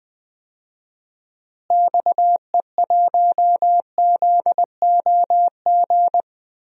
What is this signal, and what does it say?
Morse code, heard as a keyed tone, about 19 wpm, 700 Hz. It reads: XE1ZOG
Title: DXCC-100 Call Signs - 6 Characters - Repeated 3x - 20wpm